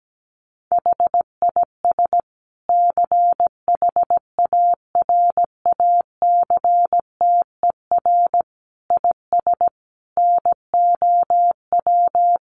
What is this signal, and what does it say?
Morse code, heard as a keyed tone, about 17 wpm, 700 Hz. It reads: HIS CHARACTER IS NOW
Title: Sentences from Top 1200 Words 17wpm